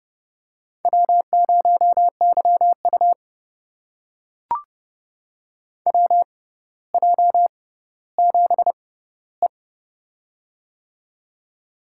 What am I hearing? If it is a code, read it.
Morse code, heard as a keyed tone, about 30 wpm, spaced as for 15 wpm, 700 Hz. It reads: W0YU WJ7E
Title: USA Call Signs - 4 Characters - 6x Character Spacing - 30wpm